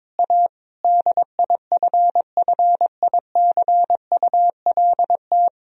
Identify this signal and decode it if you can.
Morse code, heard as a keyed tone, about 22 wpm, 700 Hz. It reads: A DIFFICULT
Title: Sentences from Top 1000 Words - Review Part 3 - 22wpm